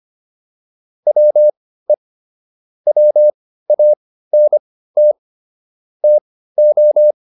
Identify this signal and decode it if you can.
Morse code, heard as a keyed tone, about 25 wpm, spaced as for 15 wpm, 600 Hz. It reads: WE WANT TO
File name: WD204 15